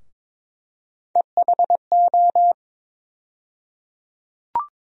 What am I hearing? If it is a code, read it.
Morse code, heard as a keyed tone, about 22 wpm, 700 Hz. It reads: EHO
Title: USA Call Sign Postfixes - 3 Letters 22wpm